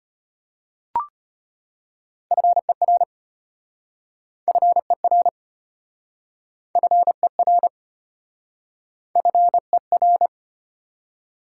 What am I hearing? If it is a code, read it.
Morse code, heard as a keyed tone, about 33 wpm, 700 Hz. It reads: FER FER FER FER